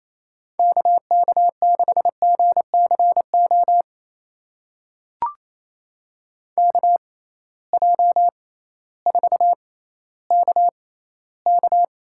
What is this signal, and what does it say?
Morse code, heard as a keyed tone, about 27 wpm, spaced as for 15 wpm, 700 Hz. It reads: KK6GCO KJ4KK
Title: USA Call Signs - 6 Characters - 6x Character Spacing - 28wpm